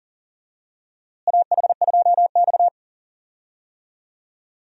Morse code, heard as a keyed tone, about 37 wpm, 700 Hz. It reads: AH2X
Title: USA Call Signs - 4 Characters - Repeated 2x - 40wpm